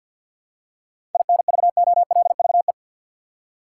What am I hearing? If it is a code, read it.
Morse code, heard as a keyed tone, about 51 wpm, 700 Hz. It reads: INVOLVE
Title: Seven Letter Words - 50wpm